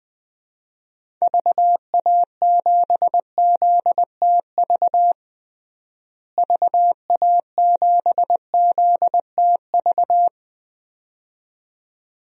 Morse code, heard as a keyed tone, about 21 wpm, 700 Hz. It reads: VA7ZTV VA7ZTV
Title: DXCC-100 Call Signs - 6 Characters - Repeated 2x - 20wpm